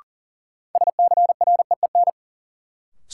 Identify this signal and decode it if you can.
Morse code, heard as a keyed tone, about 40 wpm, 700 Hz. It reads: SCREEN
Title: Sentences from Top 1000 Words - Review Part 6 - New Word Drill - 40wpm